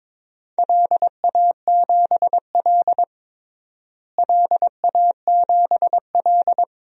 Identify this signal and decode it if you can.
Morse code, heard as a keyed tone, about 21 wpm, 700 Hz. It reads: LA7L LA7L
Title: DXCC-100 Call Signs - 4 Characters - Repeated 2x - 22wpm